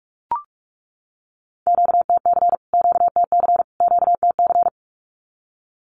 Morse code, heard as a keyed tone, about 44 wpm, 700 Hz. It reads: QTC QTC QTC
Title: Q-Codes Repeated 3x 45wpm